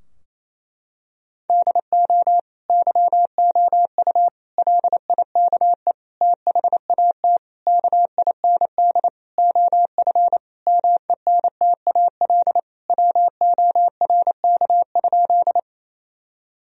Morse code, heard as a keyed tone, about 28 wpm, 700 Hz. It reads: DO YOU LIKE THAT KIND OF MENTAL WORK?
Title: Sentences from Top 1000 Words - Review Part 4 - 28wpm